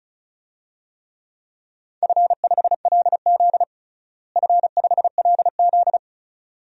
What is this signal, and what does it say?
Morse code, heard as a keyed tone, about 36 wpm, 700 Hz. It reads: F5LZ F5LZ
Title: DXCC-100 Call Signs - 4 Characters - Repeated 2x - 35wpm